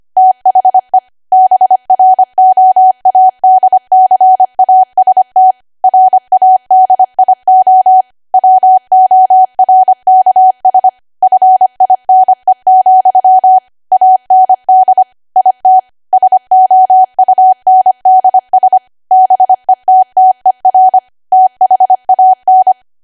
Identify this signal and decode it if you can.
Morse code, heard as a keyed tone, about 25 wpm, 750 Hz. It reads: THE BROADCAST RADIO WORKS FINE, AND IT SOUNDS BETTER THAN
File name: Archive / 25 WPM / 25WPM